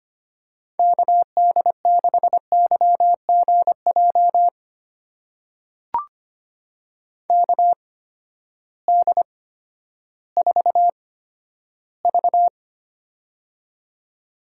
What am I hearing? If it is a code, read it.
Morse code, heard as a keyed tone, about 25 wpm, spaced as for 12 wpm, 700 Hz. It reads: KD6YGJ KD4V
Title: USA Call Signs - 8x Character Spacing - 25wpm